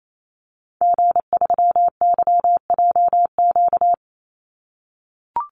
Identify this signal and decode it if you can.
Morse code, heard as a keyed tone, about 28 wpm, 700 Hz. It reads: G3YJQ
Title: Call Signs - 28wpm